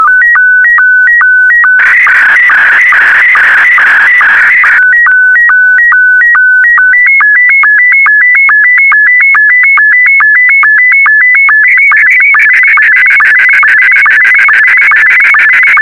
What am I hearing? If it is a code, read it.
SSTV (slow-scan television), Scottie 1.